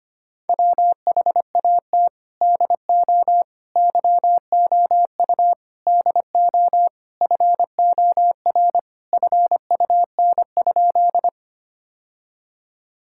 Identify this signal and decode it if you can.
Morse code, heard as a keyed tone, about 25 wpm, 700 Hz. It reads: WHAT DO YOU DO FOR FUN?
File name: practice / 25wpm / Sentences with 4-Word Phrases - 25wpm